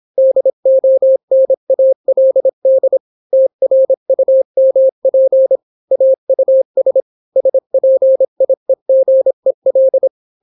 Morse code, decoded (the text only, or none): DONALD TRUMP AUS SPIEGEL